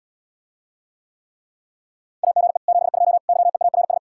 Morse code, heard as a keyed tone, about 77 wpm, 700 Hz. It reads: AGE 72 YEARS